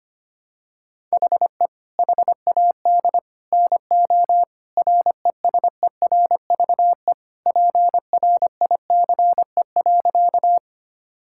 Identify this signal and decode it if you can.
Morse code, heard as a keyed tone, about 25 wpm, 700 Hz. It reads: HE HAD NO RESERVE PRICE.
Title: Sentences from Top 1000 Words - Review Part 6 - 25wpm